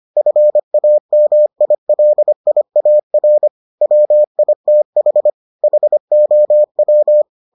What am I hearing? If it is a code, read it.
Morse code, heard as a keyed tone, about 25 wpm, 600 Hz. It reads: FAMILIAR WITH HOW